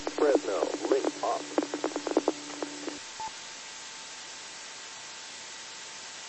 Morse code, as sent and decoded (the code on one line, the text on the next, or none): none